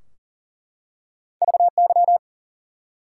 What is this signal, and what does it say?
Morse code, heard as a keyed tone, about 40 wpm, 700 Hz. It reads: VY
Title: Top 90 QSO Elements 40wpm